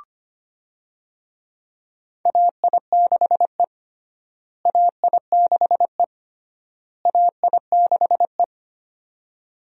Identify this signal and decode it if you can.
Morse code, heard as a keyed tone, about 25 wpm, 700 Hz. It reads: AI6E AI6E AI6E
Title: USA Call Signs - 4 Characters - Repeated 3x - 25wpm